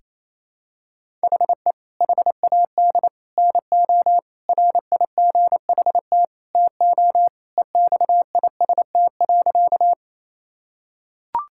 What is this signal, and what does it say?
Morse code, heard as a keyed tone, about 28 wpm, 700 Hz. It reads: HE HAD NO RIGHT TO EXIST.
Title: Sentences from Top 1000 Words - Review Part 5 - 28wpm